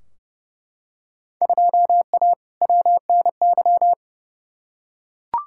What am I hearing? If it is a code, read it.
Morse code, heard as a keyed tone, about 31 wpm, 700 Hz. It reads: 2A WNY
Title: Field Day Exchanges - 30wpm